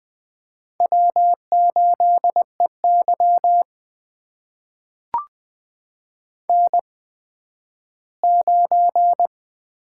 Morse code, heard as a keyed tone, about 20 wpm, spaced as for 12 wpm, 700 Hz. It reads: W8EY N9
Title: USA Call Signs - 8x Character Spacing - 20wpm